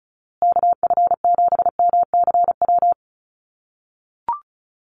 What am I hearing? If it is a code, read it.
Morse code, heard as a keyed tone, about 34 wpm, 700 Hz. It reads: KF7MCW
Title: USA Call Signs Repeated 2x 35wpm